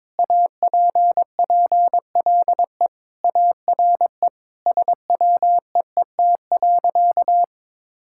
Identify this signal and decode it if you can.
Morse code, heard as a keyed tone, about 22 wpm, 700 Hz. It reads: APPLE ARE SWEET.
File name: practice / 22wpm / Sentences with 3-Word Phrases - 22wpm